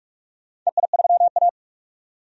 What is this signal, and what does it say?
Morse code, heard as a keyed tone, about 46 wpm, 700 Hz. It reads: EI3A
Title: DXCC-100 Call Signs - 4 Characters - 45wpm